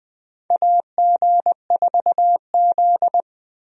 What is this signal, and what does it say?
Morse code, heard as a keyed tone, about 21 wpm, 700 Hz. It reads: AG4Z
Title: USA Call Signs - 4 Characters - Repeated 2x - 20wpm